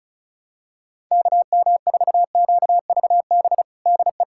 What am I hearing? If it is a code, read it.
Morse code, heard as a keyed tone, about 36 wpm, 700 Hz. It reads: KM4QVB DE
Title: USA Callsign DE Callsign - 6 Characters - 35wpm